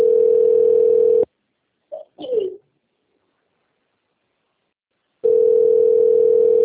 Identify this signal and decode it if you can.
Ringback tone (North America), 440 + 480 Hz.
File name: out- - - - - .35730